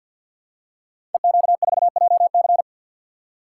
Morse code, heard as a keyed tone, about 52 wpm, 700 Hz. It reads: EQ4JC
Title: Code Groups - 5 Letter:Numbers - 50wpm